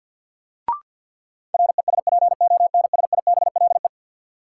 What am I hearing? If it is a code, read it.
Morse code, heard as a keyed tone, about 53 wpm, 700 Hz. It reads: RESPONSIBLE
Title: Sentences from Top 1600 Words - New Word Drill Alt - 50wpm